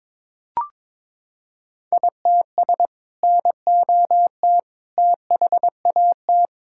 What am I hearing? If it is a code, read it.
Morse code, heard as a keyed tone, about 22 wpm, 700 Hz. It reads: ITS NOT THAT
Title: Sentences from Top 1000 Words - Review Part 5 - 22wpm